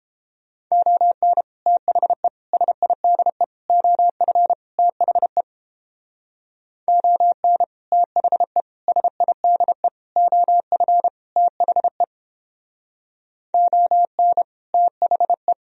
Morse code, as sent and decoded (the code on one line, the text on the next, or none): --- -. / - .... / ... .. -.. / --- ..-. / - .... / --- -. / - .... / ... .. -.. / --- ..-. / - .... / --- -. / - .... .
ON THE SIDE OF THE ON THE SIDE OF THE ON THE